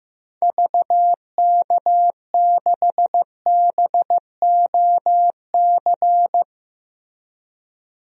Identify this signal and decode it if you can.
Morse code, heard as a keyed tone, about 15 wpm, 700 Hz. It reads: VK6BOC
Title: DXCC-100 Call Signs - 6 Characters - 15wpm